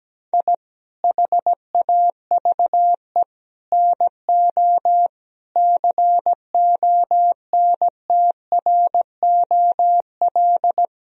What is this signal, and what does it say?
Morse code, heard as a keyed tone, about 17 wpm, 700 Hz. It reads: I HAVE NO CONTROL
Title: Sentences from Top 1000 Words - Review Part 3 - 17wpm